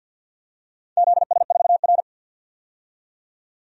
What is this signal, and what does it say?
Morse code, heard as a keyed tone, about 52 wpm, 700 Hz. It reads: ZS4R
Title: DXCC-100 Call Signs - 4 Characters - 50wpm